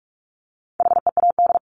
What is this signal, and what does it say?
Morse code, heard as a keyed tone, about 45 wpm, 700 Hz. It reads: HEAD
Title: Top 300 Words Repeated 45wpm